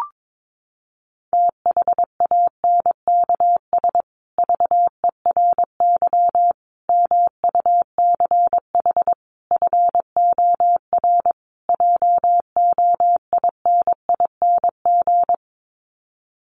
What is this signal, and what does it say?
Morse code, heard as a keyed tone, about 22 wpm, 700 Hz. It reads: THANKS VERY MUCH FOR JOINING
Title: Sets of 5 Words Encore2 22wpm